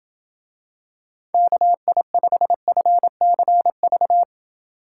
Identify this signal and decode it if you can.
Morse code, heard as a keyed tone, about 27 wpm, 700 Hz. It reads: KI5FCV